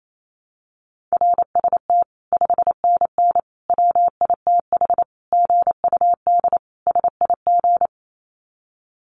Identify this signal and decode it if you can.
Morse code, heard as a keyed tone, about 28 wpm, 700 Hz. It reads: RST 5NN WITH GUD SIG
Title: Common QSO Phrases 28wpm